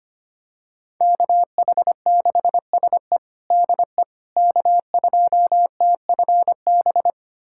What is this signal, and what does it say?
Morse code, heard as a keyed tone, about 25 wpm, 700 Hz. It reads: KH6SE DE K2TFB
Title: USA Callsign DE Callsign - 5 Characters - 25wpm